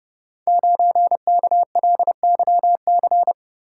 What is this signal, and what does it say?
Morse code, heard as a keyed tone, about 28 wpm, 700 Hz. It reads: 9KLYC
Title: Code Groups - 5 Letter:Numbers - 30wpm